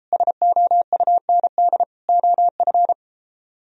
Morse code, heard as a keyed tone, about 33 wpm, 700 Hz. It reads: SOUND OF